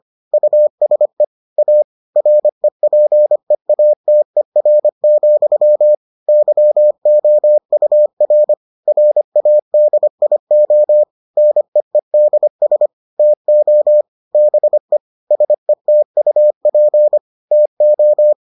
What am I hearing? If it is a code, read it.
Morse code, heard as a keyed tone, about 25 wpm, 600 Hz. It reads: USE A REPEATER, YOUR RADIO NEEDS TO BE SETUP TO